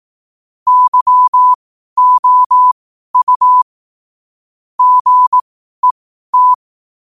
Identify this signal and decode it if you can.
Morse code, heard as a keyed tone, about 18 wpm, spaced as for 12 wpm, 1000 Hz. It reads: YOU GET
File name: qotdfiles / 12wpm